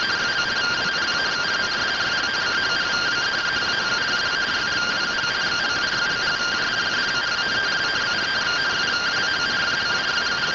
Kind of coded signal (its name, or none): two-tone teleprinter signal